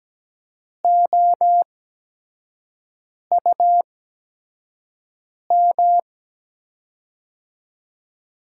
Morse code, heard as a keyed tone, about 17 wpm, spaced as for 5 wpm, 700 Hz. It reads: OUM
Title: USA Call Signs - 8x Character Spacing - 17wpm